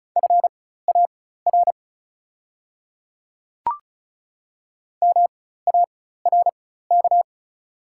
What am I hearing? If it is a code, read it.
Morse code, heard as a keyed tone, about 36 wpm, spaced as for 16 wpm, 700 Hz. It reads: FAR MARK
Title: 4x Character Spacing - Top 300 Words - 35wpm